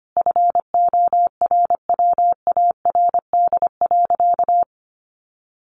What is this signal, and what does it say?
Morse code, heard as a keyed tone, about 25 wpm, 700 Hz. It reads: FORWARD.
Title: Sentences from Top 1300 Words 25wpm